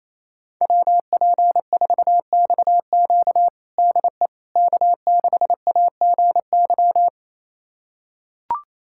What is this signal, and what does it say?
Morse code, heard as a keyed tone, about 28 wpm, 700 Hz. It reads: WP4XQ DE K6AGY
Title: USA Callsign DE Callsign 28wpm